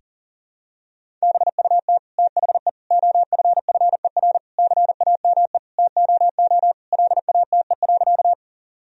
Morse code, heard as a keyed tone, about 40 wpm, 700 Hz. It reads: BUT THE OFFER CAME TOO LATE.
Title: Sentences from Top 1000 Words - Review Part 4 - 40wpm